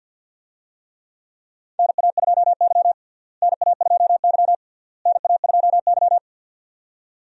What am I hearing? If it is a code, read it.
Morse code, heard as a keyed tone, about 51 wpm, 700 Hz. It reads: NA2Y NA2Y NA2Y
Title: USA Call Signs - 4 Characters - Repeated 3x - 50wpm